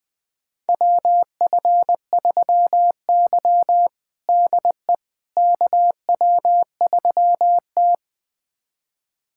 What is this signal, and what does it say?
Morse code, heard as a keyed tone, about 21 wpm, 700 Hz. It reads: WF3Y DE KW3T
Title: USA Callsign DE Callsign - 4 Characters - 20wpm